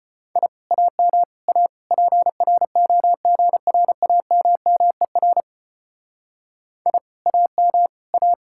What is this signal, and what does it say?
Morse code, heard as a keyed tone, about 32 wpm, 700 Hz. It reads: I AM A PROGRAMMER I AM A